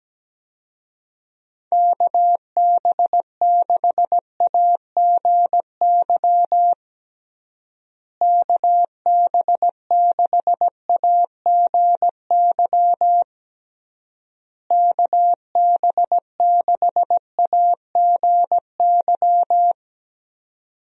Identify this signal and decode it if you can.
Morse code, heard as a keyed tone, about 17 wpm, 700 Hz. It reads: KB6AGY KB6AGY KB6AGY
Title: USA Call Signs - 6 Characters - Repeated 3x - 17wpm